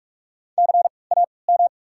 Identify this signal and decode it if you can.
Morse code, heard as a keyed tone, about 41 wpm, 690 Hz. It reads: C A M
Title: 3x Character Spacing - Top 200 Words - 45wpm